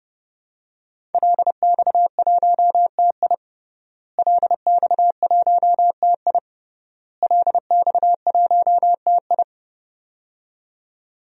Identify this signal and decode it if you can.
Morse code, heard as a keyed tone, about 30 wpm, 700 Hz. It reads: LX1TI LX1TI LX1TI
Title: DXCC-100 Call Signs - 5 Characters - Repeated 3x - 30wpm